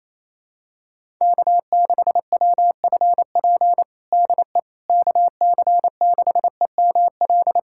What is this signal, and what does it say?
Morse code, heard as a keyed tone, about 28 wpm, 700 Hz. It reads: K6WFP DE KC6EML
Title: USA Callsign DE Callsign 28wpm